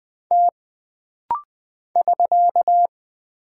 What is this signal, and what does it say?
Morse code, heard as a keyed tone, about 20 wpm, 700 Hz. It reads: T <SK>
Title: Single Character (Letter:Number:Punctuation:Prosign) - Rapid-Fire - 20wpm